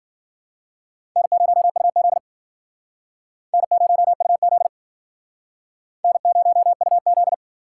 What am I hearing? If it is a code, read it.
Morse code, heard as a keyed tone, about 50 wpm, 700 Hz. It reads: N0UZ N0UZ N0UZ
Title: Speed Racing - USA Call Signs 40wpm